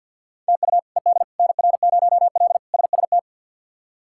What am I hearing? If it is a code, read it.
Morse code, heard as a keyed tone, about 52 wpm, 700 Hz. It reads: TU ED NR0L SST
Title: K1USN SST QSO Practice - 50wpm